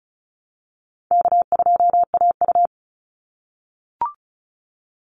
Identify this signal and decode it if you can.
Morse code, heard as a keyed tone, about 34 wpm, 700 Hz. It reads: K2AU
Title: Call Signs - 35wpm